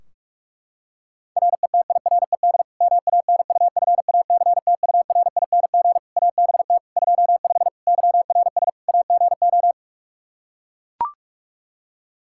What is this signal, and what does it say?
Morse code, heard as a keyed tone, about 41 wpm, 700 Hz. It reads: RETIRED MANUFACTURING ABT 25 YRS AGO
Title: K1USN SST QSO Practice 45wpm